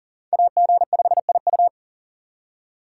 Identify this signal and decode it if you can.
Morse code, heard as a keyed tone, about 40 wpm, 700 Hz. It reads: AG5IU
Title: USA Call Signs - 5 Characters - 3x Character Spacing - 40wpm